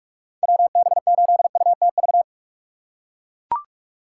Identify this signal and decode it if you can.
Morse code, heard as a keyed tone, about 43 wpm, 700 Hz. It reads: WB8UTV
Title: USA Call Signs - 6 Characters - 4x Character Spacing - 45wpm